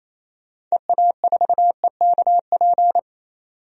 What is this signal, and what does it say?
Morse code, heard as a keyed tone, about 28 wpm, 700 Hz. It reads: EA4EKP
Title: DXCC-100 Call Signs - 6 Characters - 28wpm